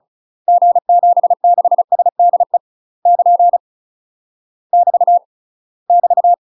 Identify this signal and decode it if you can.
Morse code, heard as a keyed tone, about 36 wpm, 700 Hz. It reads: G76SDE <KN> <BT> <BT>
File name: UK MASTER QSOs TRAINING / 35WPM / UK MASTER QSOs 0033